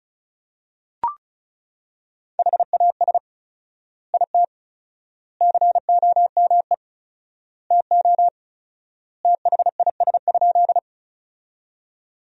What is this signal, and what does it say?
Morse code, heard as a keyed tone, about 34 wpm, 700 Hz. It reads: HAS IT COME TO THIS?
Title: 4x Word-Spacing - Sentences from Top 100 Words - 35wpm